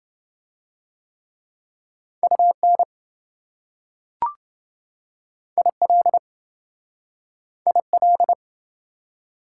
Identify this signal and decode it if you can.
Morse code, heard as a keyed tone, about 26 wpm, 700 Hz. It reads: UN IL IL